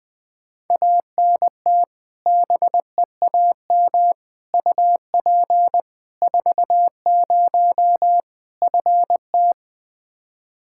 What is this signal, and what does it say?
Morse code, heard as a keyed tone, about 21 wpm, 700 Hz. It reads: ANT BEAM UP 40 FT
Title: Practice QSO - Exchange 2 - Info - 20wpm